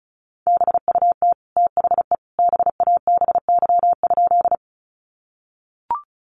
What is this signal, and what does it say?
Morse code, heard as a keyed tone, about 34 wpm, 700 Hz. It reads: BUT THE BABY?
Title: Sentences from Top 1300 Words 35wpm